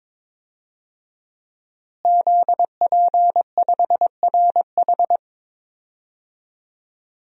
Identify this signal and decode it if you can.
Morse code, heard as a keyed tone, about 22 wpm, 700 Hz. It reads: ZP5RH
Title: DXCC-100 Call Signs - 5 Characters - Repeated 3x - 22wpm